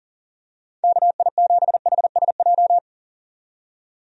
Morse code, heard as a keyed tone, about 39 wpm, 700 Hz. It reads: KI7HSJ
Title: USA Call Signs - 6 Characters - 40wpm